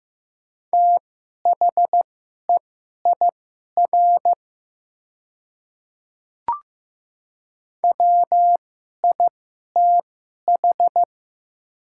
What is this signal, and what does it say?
Morse code, heard as a keyed tone, about 15 wpm, spaced as for 11 wpm, 700 Hz. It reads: THEIR WITH
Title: 2x Character Spacing - Top 100 Words - 15wpm